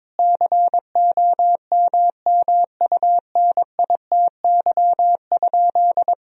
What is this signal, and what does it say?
Morse code, heard as a keyed tone, about 22 wpm, 700 Hz. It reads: COMMUNITY?
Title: Sentences from Top 1000 Words - Review Part 4 - 22wpm